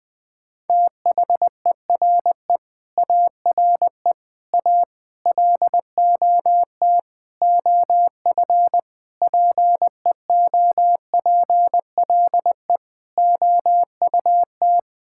Morse code, heard as a keyed tone, about 21 wpm, 700 Hz. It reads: THERE ARE A LOT OF PEOPLE OUT
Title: Sentences with 8-Word Phrases - 20wpm